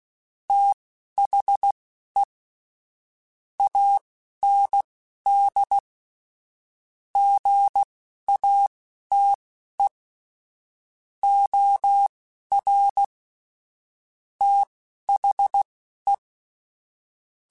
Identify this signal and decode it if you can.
Morse code, heard as a keyed tone, about 16 wpm, spaced as for 11 wpm, 800 Hz. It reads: THE AND GATE OR THE
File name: Technical Subjects / Technical Subjects-mp3s / AND Gate @10